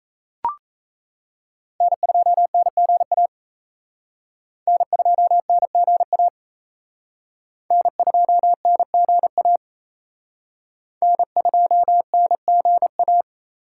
Morse code, heard as a keyed tone, about 34 wpm, 700 Hz. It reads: N2NGA N2NGA N2NGA N2NGA
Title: Speed Racing - USA 1x3 Call Signs 28wpm